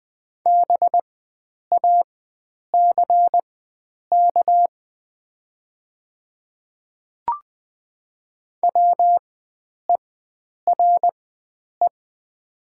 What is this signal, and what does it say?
Morse code, heard as a keyed tone, about 20 wpm, spaced as for 9 wpm, 700 Hz. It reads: BACK WERE
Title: 4x Character Spacing - Top 300 Words - 20wpm